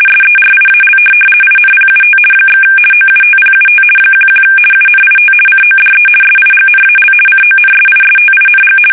RTTY, about 50 baud. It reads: SERIAL INPUT / OUTPUT 11 UNIT CODE SIGNAL CONSISTING
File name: fsk850-50B / teletype-corp